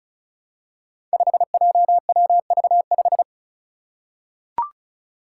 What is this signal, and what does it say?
Morse code, heard as a keyed tone, about 34 wpm, 700 Hz. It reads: 5JWV5
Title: Five Challenging Letter-Number Combinations 35wpm